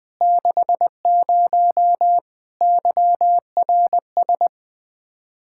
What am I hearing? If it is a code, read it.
Morse code, heard as a keyed tone, about 20 wpm, 700 Hz. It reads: 60 YRS
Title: Practice QSO - Exchange 3 - More Info - 20wpm